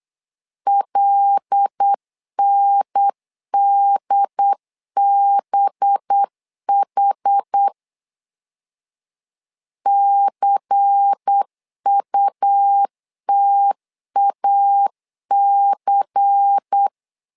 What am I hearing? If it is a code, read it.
Morse code, heard as a keyed tone, about 8 wpm, 800 Hz. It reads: LNDBH CUTAC
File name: LEZIONI / 5 LBVC / 5 LBVC 8wpm 2935